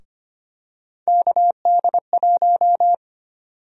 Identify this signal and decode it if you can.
Morse code, heard as a keyed tone, about 25 wpm, 700 Hz. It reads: KD1
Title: USA Call Sign Prefixes - Repeated 2x - 25wpm